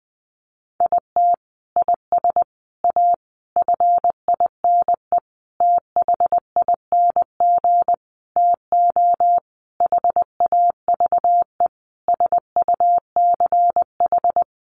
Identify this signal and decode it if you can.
Morse code, heard as a keyed tone, about 20 wpm, 700 Hz. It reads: IT IS A FINE THING TO HAVE SUCH
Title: Sentences from Top 1000 Words 20wpm